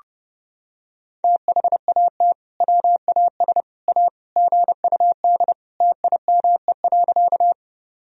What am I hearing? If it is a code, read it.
Morse code, heard as a keyed tone, about 32 wpm, 700 Hz. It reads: THAT WAS A GUD TIME.
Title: Sentences from Top 200 Words and Abbreviations 30wpm